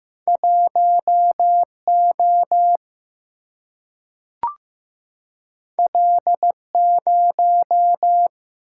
Morse code, heard as a keyed tone, about 15 wpm, 700 Hz. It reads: 1 O L 0
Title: Single-Letter Single-Number 15wpm